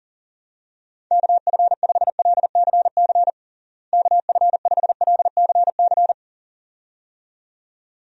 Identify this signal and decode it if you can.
Morse code, heard as a keyed tone, about 40 wpm, 700 Hz. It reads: KF5LCC KF5LCC
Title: USA Call Signs - 6 Characters - Repeated 2x - 40wpm